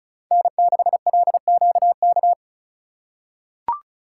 Morse code, heard as a keyed tone, about 36 wpm, 700 Hz. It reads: N6LQK
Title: USA Call Signs - 5 Characters - 4x Character Spacing - 35wpm